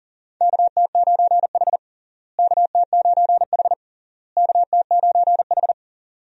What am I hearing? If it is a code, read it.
Morse code, heard as a keyed tone, about 40 wpm, 700 Hz. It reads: KT9H KT9H KT9H
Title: USA Call Signs - 4 Characters - Repeated 3x - 40wpm